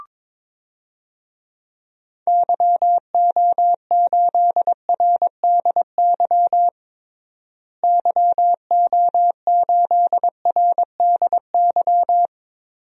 Morse code, heard as a keyed tone, about 22 wpm, 700 Hz. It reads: YO8RDY YO8RDY
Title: DXCC-100 Call Signs - 6 Characters - Repeated 3x - 22wpm